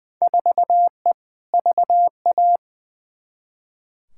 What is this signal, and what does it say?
Morse code, heard as a keyed tone, about 20 wpm, 700 Hz. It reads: 4E VA
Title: Field Day Exchanges Encore - 20wpm